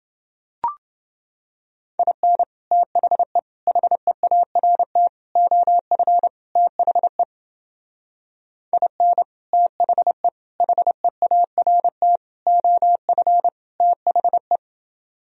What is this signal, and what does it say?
Morse code, heard as a keyed tone, about 29 wpm, 700 Hz. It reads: IN THE HEART OF THE IN THE HEART OF THE